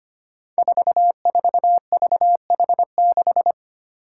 Morse code, heard as a keyed tone, about 25 wpm, 700 Hz. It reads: 44VH6
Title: Five Challenging Letter-Number Combinations 25wpm